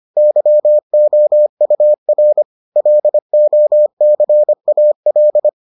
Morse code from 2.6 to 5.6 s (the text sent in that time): LOCAL